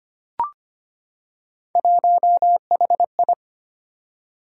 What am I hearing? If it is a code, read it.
Morse code, heard as a keyed tone, about 25 wpm, 700 Hz. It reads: 1HI
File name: Three Letter-Number Combinations - 25wpm